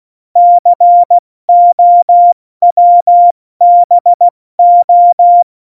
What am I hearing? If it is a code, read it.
Morse code, heard as a keyed tone, about 16 wpm, 700 Hz. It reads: COWBO
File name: Morsekurs / Witze / Witz 06. Oktober 2020, 70 F80